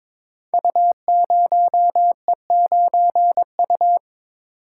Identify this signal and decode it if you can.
Morse code, heard as a keyed tone, about 22 wpm, 700 Hz. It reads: U0E9U
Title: Code Groups - 5 Letter:Numbers - 22wpm